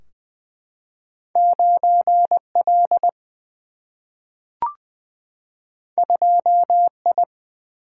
Morse code, heard as a keyed tone, about 20 wpm, 700 Hz. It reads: 9L 2I
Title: Single-Letter Single-Number 20wpm